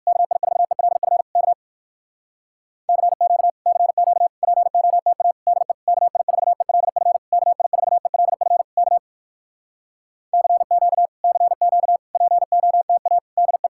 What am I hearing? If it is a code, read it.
Morse code, heard as a keyed tone, about 66 wpm, 700 Hz. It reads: KI4ELF K CQ CQ POTA DE KI4ELF KI4ELF K CQ CQ POTA DE